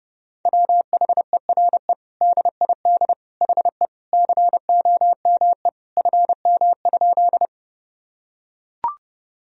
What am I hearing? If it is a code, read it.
Morse code, heard as a keyed tone, about 30 wpm, 700 Hz. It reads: WHERE DID HE COME FM?
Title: Sentences from Top 200 Words and Abbreviations 30wpm